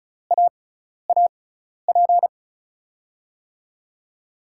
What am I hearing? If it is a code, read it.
Morse code, heard as a keyed tone, about 34 wpm, spaced as for 12 wpm, 700 Hz. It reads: AAP
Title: USA Call Signs - 5 Characters - 6x Character Spacing - 35wpm